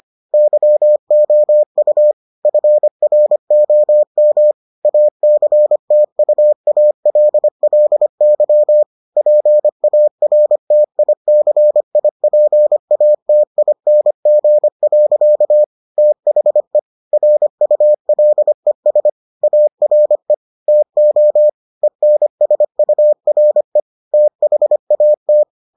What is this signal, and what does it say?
Morse code, heard as a keyed tone, about 25 wpm, 600 Hz. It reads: YOU FROM ACTUALLY PARTICIPATING. THE RULES ARE TO ENSURE THAT